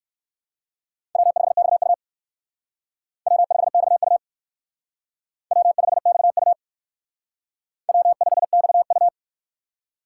Morse code, heard as a keyed tone, about 59 wpm, 700 Hz. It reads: W5XU W5XU W5XU W5XU